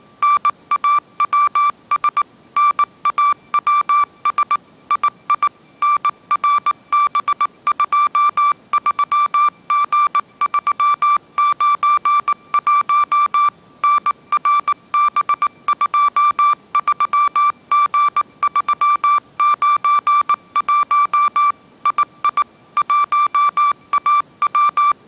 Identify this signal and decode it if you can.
Morse code, heard as a keyed tone, about 19 wpm, 1170 Hz. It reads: NAWS NAWS II NRB23G391 NRB23G391 II 1AW
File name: 20wpm